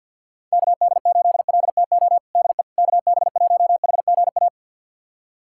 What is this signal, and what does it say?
Morse code, heard as a keyed tone, about 45 wpm, 700 Hz. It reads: KD8LTO DE KB1HGA
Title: USA Callsign DE Callsign - 6 Characters - 50wpm